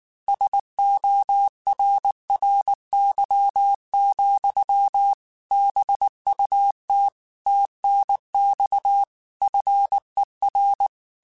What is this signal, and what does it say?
Morse code, heard as a keyed tone, about 19 wpm, 800 Hz. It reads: SORRY, BUT TNX FER
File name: QSOs / QSOs-mp3s / WA7QOX 2@18